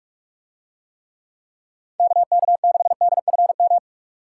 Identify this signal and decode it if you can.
Morse code, heard as a keyed tone, about 42 wpm, 700 Hz. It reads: KK6DFM